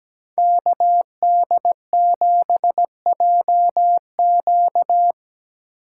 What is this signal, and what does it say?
Morse code, heard as a keyed tone, about 17 wpm, 700 Hz. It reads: KD7JQ
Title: USA Call Signs - 5 Characters - 8x Character Spacing - 17wpm